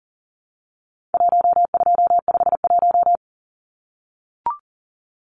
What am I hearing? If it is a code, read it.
Morse code, heard as a keyed tone, about 40 wpm, 700 Hz. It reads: 1251
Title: Numbers - Contest Style - 4-Digit 40wpm